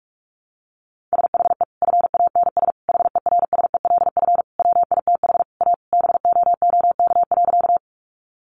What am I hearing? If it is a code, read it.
Morse code, heard as a keyed tone, about 44 wpm, 700 Hz. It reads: SHE FANS HERSELF WITH A BOOK.
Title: Easy Phrases 45wpm